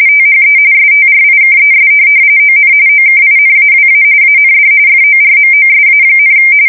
RTTY, about 45.45 baud. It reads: THE STRIKES ON OKINAWA AND FORMOSA,
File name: fsk170 / USS-Massachusetts